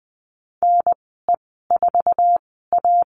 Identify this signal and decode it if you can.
Morse code, heard as a keyed tone, about 20 wpm, spaced as for 15 wpm, 700 Hz. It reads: NE4A
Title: 069 Lesson 12g - Letter d - Call Signs 20wpm